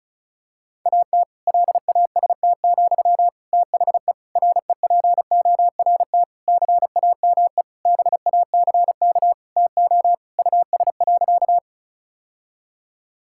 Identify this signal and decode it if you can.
Morse code, heard as a keyed tone, about 36 wpm, 700 Hz. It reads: AT LAST, THE REPORT CAME BACK TO US.
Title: Sentences from Top 1000 Words - Review Part 6 - 35wpm